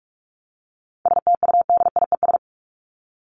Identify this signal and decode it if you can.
Morse code, heard as a keyed tone, about 44 wpm, 700 Hz. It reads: STUDIES
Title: ES:S Suffix - Top 100 Words with Base Words 45wpm